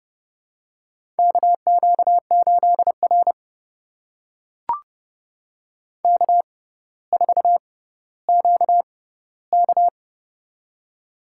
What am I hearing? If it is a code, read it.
Morse code, heard as a keyed tone, about 31 wpm, spaced as for 16 wpm, 700 Hz. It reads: KQ8R K4QK
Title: USA Call Signs - 6x Character Spacing - 30wpm